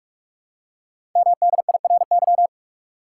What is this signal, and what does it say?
Morse code, heard as a keyed tone, about 44 wpm, 700 Hz. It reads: MDIRY
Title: Code Groups - 5 Letters - 45wpm